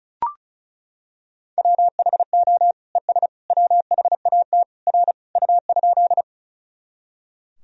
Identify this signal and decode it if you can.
Morse code, heard as a keyed tone, about 36 wpm, 700 Hz. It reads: WHO ES WHAT R U?
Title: Sentences from Top 100 Words and Abbreviations 35wpm